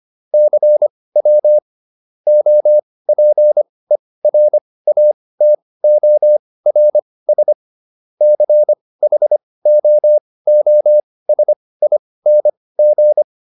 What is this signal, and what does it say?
Morse code, heard as a keyed tone, about 25 wpm, spaced as for 18 wpm, 600 Hz. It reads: CW OPERATORS CHOOSING